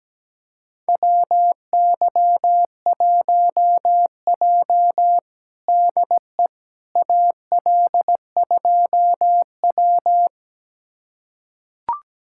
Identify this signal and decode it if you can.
Morse code, heard as a keyed tone, about 17 wpm, 700 Hz. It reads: WY1J DE AL2W
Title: USA Callsign DE Callsign - 4 Characters - 17wpm